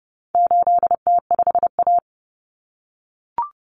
Morse code, heard as a keyed tone, about 30 wpm, 700 Hz. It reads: 8T5A
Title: Common Cut Numbers - 4-Digit 30wpm